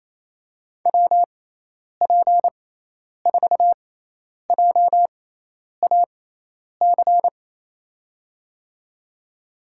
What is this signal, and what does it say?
Morse code, heard as a keyed tone, about 27 wpm, spaced as for 10 wpm, 700 Hz. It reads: WP4JAC
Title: USA Call Signs - 6 Characters - 6x Character Spacing - 28wpm